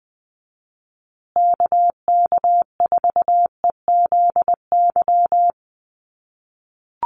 Morse code, heard as a keyed tone, about 20 wpm, 700 Hz. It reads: KK4EZY
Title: USA Call Signs Repeated 2x 20wpm